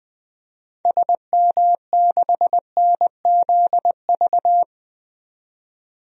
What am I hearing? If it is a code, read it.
Morse code, heard as a keyed tone, about 20 wpm, 700 Hz. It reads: SM6NZV
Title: DXCC-100 Call Signs - 6 Characters - 20wpm